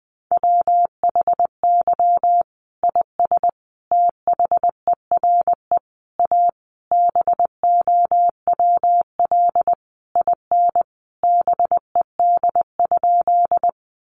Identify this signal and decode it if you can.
Morse code, heard as a keyed tone, about 20 wpm, 700 Hz. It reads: WHY IS THERE A BOWL IN BED?
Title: Sentences from Top 1200 Words 20wpm